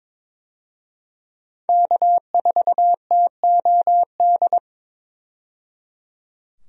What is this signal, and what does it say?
Morse code, heard as a keyed tone, about 21 wpm, 700 Hz. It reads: K4TOD
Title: USA Call Signs - 5 Characters - 22wpm